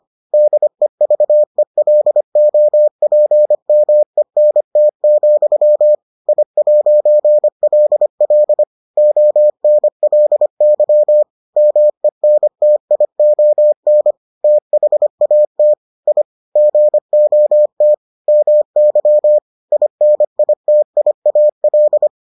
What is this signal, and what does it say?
Morse code, heard as a keyed tone, about 25 wpm, 600 Hz. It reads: DEVELOPMENT, I'LL ONLY MENTION THAT I GOT MY INITIAL